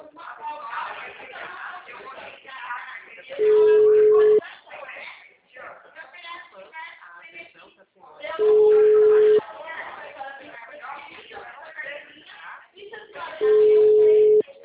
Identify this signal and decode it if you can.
Ringback tone (Europe, 425 Hz), with speech.
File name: out- -2007- - - .83878